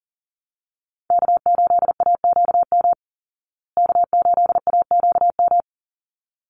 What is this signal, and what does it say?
Morse code, heard as a keyed tone, about 40 wpm, 700 Hz. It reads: K8AQM K8AQM
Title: Call Signs Repeated 3x - 40wpm